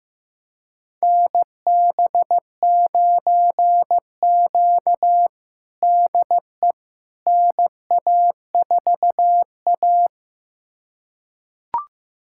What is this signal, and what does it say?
Morse code, heard as a keyed tone, about 15 wpm, 700 Hz. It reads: NB9Q DE NA4A
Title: USA Callsign DE Callsign - 4 Characters - 15wpm